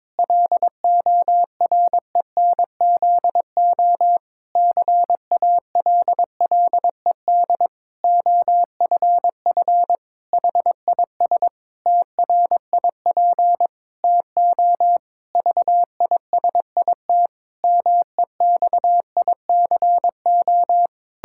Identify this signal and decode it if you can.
Morse code, heard as a keyed tone, about 22 wpm, 700 Hz. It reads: LORENZO CALLED OFF HIS TRIP TO VISIT MEXICO
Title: Pangram Challenge v2 - 22wpm